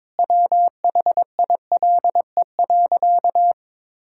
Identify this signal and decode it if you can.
Morse code, heard as a keyed tone, about 22 wpm, 700 Hz. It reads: WHILE.
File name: practice / 22wpm / Sentences with 4-Word Phrases - 22wpm